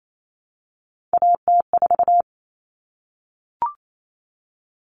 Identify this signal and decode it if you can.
Morse code, heard as a keyed tone, about 28 wpm, 700 Hz. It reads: AT4
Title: Common Cut Numbers - 3-Digit 28wpm